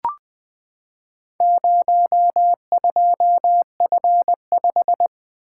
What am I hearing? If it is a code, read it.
Morse code, heard as a keyed tone, about 20 wpm, 700 Hz. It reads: 02F5
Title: Four Letter-Number Combinations - 20wpm